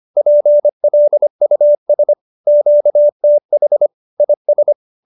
Morse code, heard as a keyed tone, about 25 wpm, 590 Hz. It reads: PLUS QTH IS